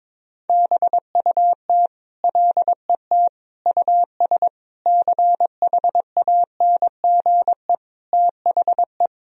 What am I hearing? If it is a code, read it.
Morse code, heard as a keyed tone, about 22 wpm, 700 Hz. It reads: BUT LET US CHANGE THE
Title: Sentences from Top 1000 Words - Review Part 1 - 22wpm